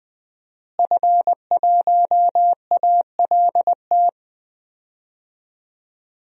Morse code, heard as a keyed tone, about 20 wpm, 700 Hz. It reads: F1ALT
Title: DXCC-100 Call Signs - 5 Characters - Repeated 3x - 20wpm